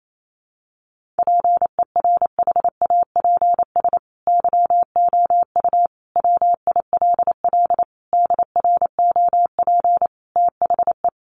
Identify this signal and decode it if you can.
Morse code, heard as a keyed tone, about 28 wpm, 700 Hz. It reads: PERHAPS YOU WILL DROP THE
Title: Sentences from Top 1400 Words 28wpm